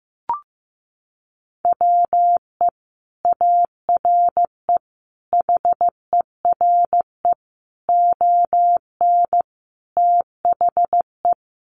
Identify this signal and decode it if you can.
Morse code, heard as a keyed tone, about 15 wpm, 700 Hz. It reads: WE ARE HERE ON THE
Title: Sentences from Top 1400 Words 15wpm